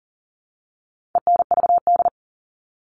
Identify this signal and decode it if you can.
Morse code, heard as a keyed tone, about 40 wpm, 700 Hz. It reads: ENVD
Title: All Cut Numbers - 4-Digit 40wpm